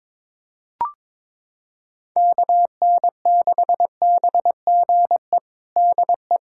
Morse code, heard as a keyed tone, about 22 wpm, 700 Hz. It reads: KN6BGE DE
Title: USA Callsign DE Callsign 22wpm